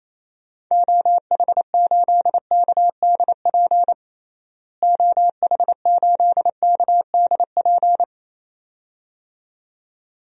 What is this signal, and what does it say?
Morse code, heard as a keyed tone, about 28 wpm, 700 Hz. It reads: OH8KDP OH8KDP
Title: DXCC-100 Call Signs - 6 Characters - Repeated 2x - 28wpm